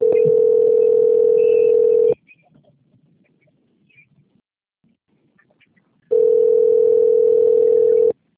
Ringback tone (North America), 440 + 480 Hz.
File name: out- - - - - .29209